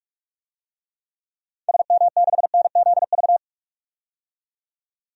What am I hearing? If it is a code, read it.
Morse code, heard as a keyed tone, about 46 wpm, 700 Hz. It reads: SM6NZV
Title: DXCC-100 Call Signs - 6 Characters - 45wpm